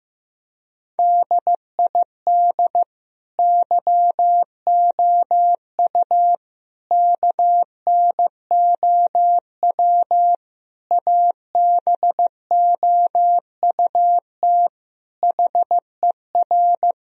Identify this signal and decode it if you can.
Morse code, heard as a keyed tone, about 15 wpm, 700 Hz. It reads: DID YOU KNOW ABOUT HER
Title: Sentences from Top 1000 Words - Review Part 4 - 15wpm